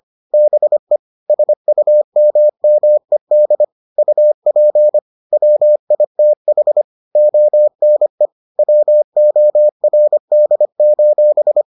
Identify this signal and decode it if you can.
Morse code, heard as a keyed tone, about 25 wpm, 600 Hz. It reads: BE SUMMED UP WITH ONE WORD: